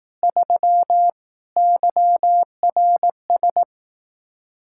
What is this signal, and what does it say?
Morse code, heard as a keyed tone, about 18 wpm, 700 Hz. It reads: 3 YRS
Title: Speed-Racing Common QSO Phrases Encore - 15wpm